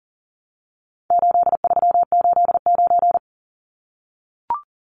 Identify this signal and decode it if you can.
Morse code, heard as a keyed tone, about 39 wpm, 700 Hz. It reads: 8389
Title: Numbers - 4-Digit 40wpm